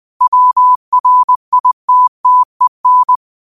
Morse code, heard as a keyed tone, about 20 wpm, 1000 Hz. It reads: WRITTEN